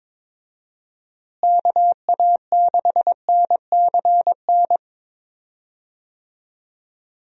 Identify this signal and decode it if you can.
Morse code, heard as a keyed tone, about 21 wpm, 700 Hz. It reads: KA6NCN
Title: USA Call Signs - 6 Characters - Repeated 2x - 22wpm